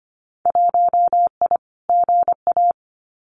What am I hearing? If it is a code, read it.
Morse code, heard as a keyed tone, about 25 wpm, 700 Hz. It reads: 1I GA
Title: Winter Field Day Exchanges Encore 25wpm